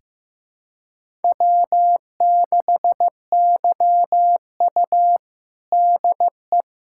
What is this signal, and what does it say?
Morse code, heard as a keyed tone, about 15 wpm, 700 Hz. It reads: W6YU DE
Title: USA Callsign DE Callsign - 4 Characters - 15wpm